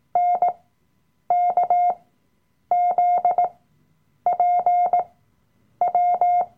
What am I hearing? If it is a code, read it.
Morse code, heard as a keyed tone, about 18 wpm, spaced as for 8 wpm, 710 Hz. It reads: NKZPW
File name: Workshop Dateien / CW-Online-Kurs / Lektion 007